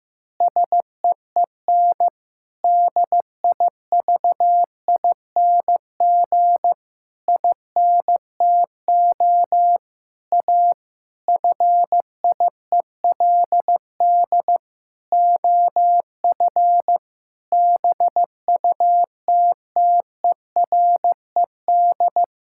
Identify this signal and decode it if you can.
Morse code, heard as a keyed tone, about 15 wpm, 700 Hz. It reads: SEEN DIVING INTO A FIELD OF BUTTERED
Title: Pangram Challenge v2 - 15wpm